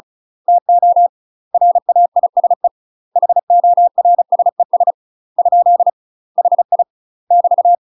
Morse code, heard as a keyed tone, about 34 wpm, 700 Hz. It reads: TO RAISE HORSES ? HI <BT>
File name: UK MASTER QSOs TRAINING/35WPM